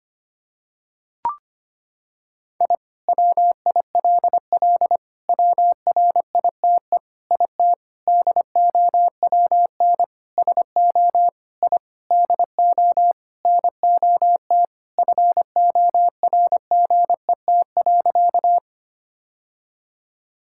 Morse code, heard as a keyed tone, about 25 wpm, 700 Hz. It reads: I WILL WRITE IT DOWN SO I DO NOT FORGET.
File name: practice / 25wpm / Sentences with 4-Word Phrases - 25wpm